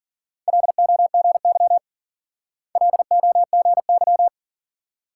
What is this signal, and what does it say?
Morse code, heard as a keyed tone, about 48 wpm, 700 Hz. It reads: LOGY LOGY
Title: Speed-Racing Multi-letter Phonemes - 40wpm